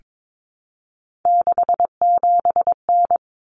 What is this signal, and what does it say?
Morse code, heard as a keyed tone, about 22 wpm, 700 Hz. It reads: 67N
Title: Common Cut Numbers - 3-Digit 22wpm